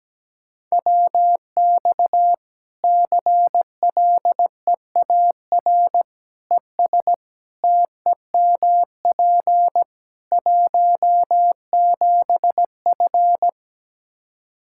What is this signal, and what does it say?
Morse code, heard as a keyed tone, about 17 wpm, 700 Hz. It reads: WX CLEAR ES TEMP 17F
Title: Practice QSO - Exchange 2 - Info - 17wpm